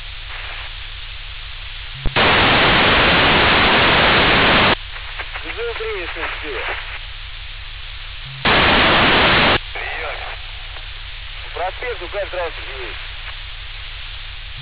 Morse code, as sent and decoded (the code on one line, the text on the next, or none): none